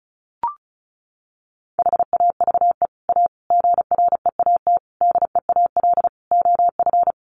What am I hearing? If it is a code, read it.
Morse code, heard as a keyed tone, about 34 wpm, 700 Hz. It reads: HAVE A GREAT DEAL OF
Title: Sets of 5 Words Encore2 35wpm